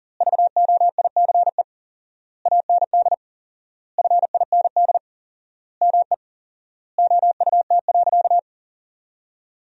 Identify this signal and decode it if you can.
Morse code, heard as a keyed tone, about 40 wpm, 700 Hz. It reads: VOICE AND FIND ME OUT.
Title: 4x Word-Spacing - Sentences from Top 500 Words - 40wpm